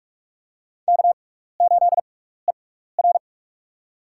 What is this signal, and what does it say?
Morse code, heard as a keyed tone, about 44 wpm, spaced as for 15 wpm, 700 Hz. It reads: K8ER
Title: USA Call Signs - 4 Characters - 6x Character Spacing - 45wpm